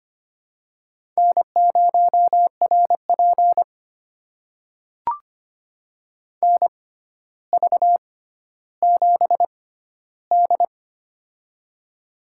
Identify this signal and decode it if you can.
Morse code, heard as a keyed tone, about 25 wpm, spaced as for 13 wpm, 700 Hz. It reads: N0RP NV7D
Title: USA Call Signs - 4 Characters - 6x Character Spacing - 25wpm